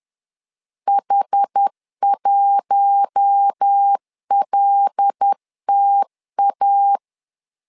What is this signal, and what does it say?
Morse code, heard as a keyed tone, about 10 wpm, 800 Hz. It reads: H1LTA
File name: LEZIONI / 5 LBVC / 5 L 10wpm 2259